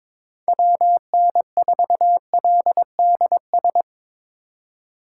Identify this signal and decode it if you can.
Morse code, heard as a keyed tone, about 25 wpm, 700 Hz. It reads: WN4LDS
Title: Speed Racing - USA Call Signs 22wpm